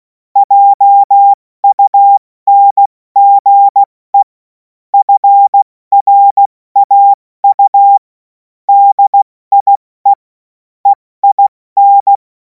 Morse code, heard as a keyed tone, about 16 wpm, spaced as for 14 wpm, 800 Hz. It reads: JUNGE FRAU DIE EIN
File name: Morsekurs / Witze / Witz 03. Februar 2026 70 F80